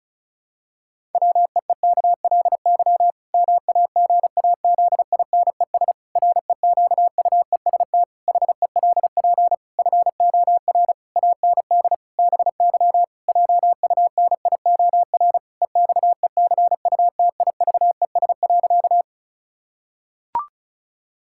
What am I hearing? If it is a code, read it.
Morse code, heard as a keyed tone, about 36 wpm, 700 Hz. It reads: WEEKLY MAGAZINES REQUEST HELP FOR AND BY JUNIOR EXECUTIVES.
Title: Pangram Challenge v2 - 35wpm